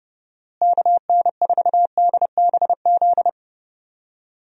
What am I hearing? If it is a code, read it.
Morse code, heard as a keyed tone, about 30 wpm, 700 Hz. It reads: KN4DBZ
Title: USA Call Signs - 6 Characters - Repeated 2x - 30wpm